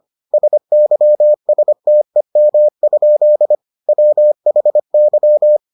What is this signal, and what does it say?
Morse code, heard as a keyed tone, about 25 wpm, 600 Hz. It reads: SYSTEM? WHY